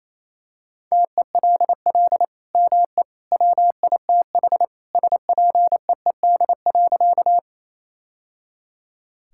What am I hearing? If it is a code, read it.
Morse code, heard as a keyed tone, about 27 wpm, 700 Hz. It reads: TELL ME WITH SPEED.
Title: Sentences from Top 1000 Words - Review Part 5 - 28wpm